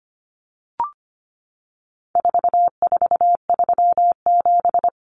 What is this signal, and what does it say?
Morse code, heard as a keyed tone, about 25 wpm, 700 Hz. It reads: 4437
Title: Numbers - 4-Digit 25wpm